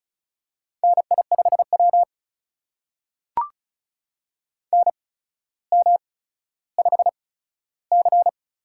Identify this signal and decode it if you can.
Morse code, heard as a keyed tone, about 36 wpm, spaced as for 15 wpm, 700 Hz. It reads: NI5W NM5C
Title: USA Call Signs - 4 Characters - 8x Character Spacing - 35wpm